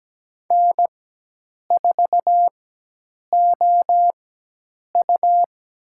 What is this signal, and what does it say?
Morse code, heard as a keyed tone, about 17 wpm, spaced as for 8 wpm, 700 Hz. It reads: N4OU
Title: USA Call Signs - 4x Character Spacing - 17wpm